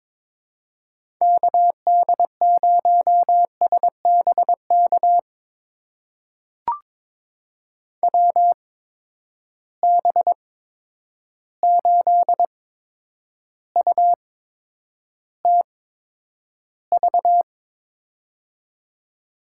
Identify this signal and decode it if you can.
Morse code, heard as a keyed tone, about 22 wpm, spaced as for 9 wpm, 700 Hz. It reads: KD0SBK WB8UTV
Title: USA Call Signs - 6 Characters - 8x Character Spacing - 22wpm